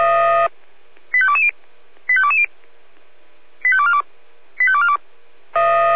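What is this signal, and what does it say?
Five-tone selective call (ZVEI), heard as quick runs of five tones, tones of about 70 ms. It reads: 85220 85220 85242 85242
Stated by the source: ZVEI five-tone selective call received by SDR.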